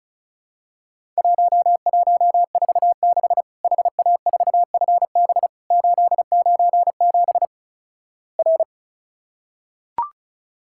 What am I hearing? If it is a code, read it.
Morse code, heard as a keyed tone, about 36 wpm, 700 Hz. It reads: 1146 HA4FB 897 R
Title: Worked All Europe DX Contest - QTC Exchange - 35wpm